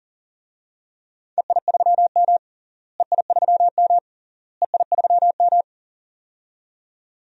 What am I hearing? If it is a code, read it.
Morse code, heard as a keyed tone, about 40 wpm, 700 Hz. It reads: EI3M EI3M EI3M
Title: DXCC-100 Call Signs - 4 Characters - Repeated 3x - 40wpm